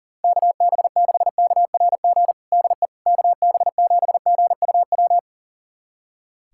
Morse code, heard as a keyed tone, about 40 wpm, 690 Hz. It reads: KB6KRG DE KB7GUW
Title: USA Callsign DE Callsign - 6 Characters - 40wpm